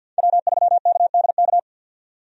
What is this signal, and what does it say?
Morse code, heard as a keyed tone, about 52 wpm, 700 Hz. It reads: W3KDK
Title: USA Call Signs - 5 Characters - 50wpm